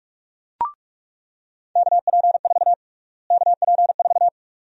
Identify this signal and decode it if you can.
Morse code, heard as a keyed tone, about 45 wpm, 700 Hz. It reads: KP4 KP4
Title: USA Call Sign Prefixes - Repeated 2x - 45wpm